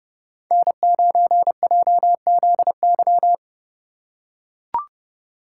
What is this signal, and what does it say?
Morse code, heard as a keyed tone, about 30 wpm, 700 Hz. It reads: N9JZY
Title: K1USN SST QSO Practice - 30wpm